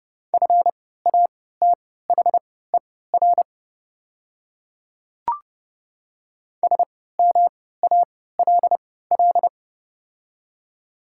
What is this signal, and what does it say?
Morse code, heard as a keyed tone, about 30 wpm, spaced as for 17 wpm, 690 Hz. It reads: FATHER SMALL
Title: 3x Character Spacing - Top 500 Words - 30wpm